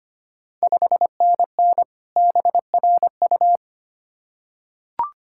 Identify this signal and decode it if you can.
Morse code, heard as a keyed tone, about 25 wpm, 700 Hz. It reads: 5NN BRU
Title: Ontario QSO Party Exchange 25wpm